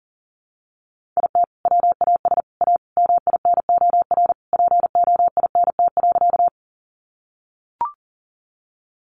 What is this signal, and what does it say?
Morse code, heard as a keyed tone, about 40 wpm, 700 Hz. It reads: IT WAS A MINOR POINT.
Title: Sentences from Top 1500 Words 40wpm